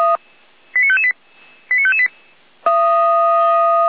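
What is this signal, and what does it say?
Five-tone selective call (ZVEI), heard as quick runs of five tones, tones of about 70 ms. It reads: 89558 89558
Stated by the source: Five-tone ZVEI selective call received by SDR, served from audio.f85360.feuerwehr-orlamuende.org (fire brigade Orlamünde).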